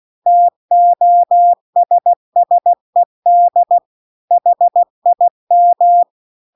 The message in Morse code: TOSSED HIM

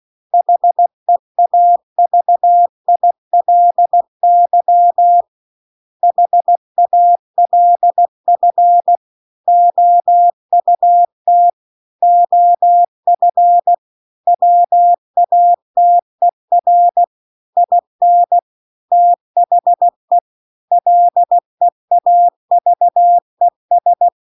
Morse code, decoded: HEAVILY HALF OUT OF WATER IN THE LEAVES

